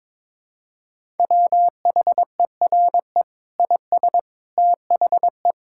Morse code, heard as a keyed tone, about 22 wpm, 700 Hz. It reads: WHERE IS THE
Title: Sentences from Top 1000 Words - Review Part 6 - 22wpm